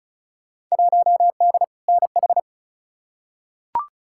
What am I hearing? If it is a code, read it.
Morse code, heard as a keyed tone, about 36 wpm, 700 Hz. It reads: 1D NH
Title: Field Day Exchanges Encore - 35wpm